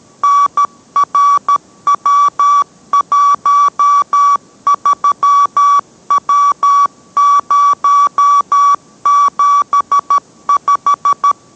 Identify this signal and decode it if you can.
Morse code, heard as a keyed tone, about 13 wpm, 1140 Hz. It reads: NRW13W075